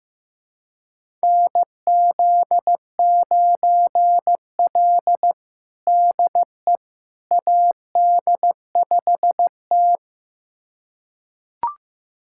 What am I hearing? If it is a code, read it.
Morse code, heard as a keyed tone, about 15 wpm, 700 Hz. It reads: NZ9L DE AD5T
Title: USA Callsign DE Callsign - 4 Characters - 15wpm